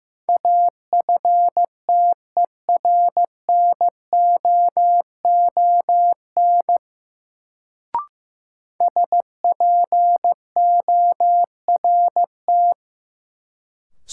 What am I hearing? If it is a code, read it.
Morse code, heard as a keyed tone, about 16 wpm, 700 Hz. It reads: AFTERNOON SPORT
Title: Sentences from Top 1500 Words - New Word Drill Alt - 15wpm